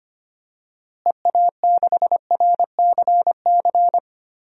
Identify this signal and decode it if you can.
Morse code, heard as a keyed tone, about 25 wpm, 700 Hz. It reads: EA6RCC
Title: DXCC-100 Call Signs - 6 Characters - Repeated 3x - 25wpm